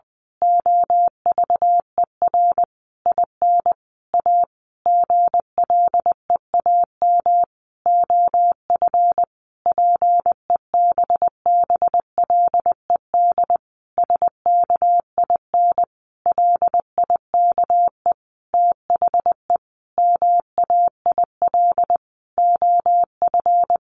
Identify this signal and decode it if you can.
Morse code, heard as a keyed tone, about 20 wpm, 700 Hz. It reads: OVER IN A GLEAM OF PEBBLED SKIN LIKE THE MAIL OF
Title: A Sound Of Thunder 20wpm